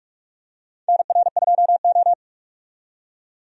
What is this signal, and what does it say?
Morse code, heard as a keyed tone, about 46 wpm, 700 Hz. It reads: NR2O
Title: USA Call Signs - 4 Characters - 45wpm